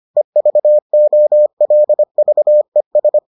Morse code, heard as a keyed tone, about 25 wpm, 600 Hz. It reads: EVOLVES